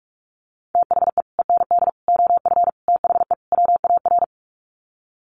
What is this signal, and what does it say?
Morse code, heard as a keyed tone, about 44 wpm, 700 Hz. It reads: THE END OF THE WAR
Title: Sets of 5 Words 45wpm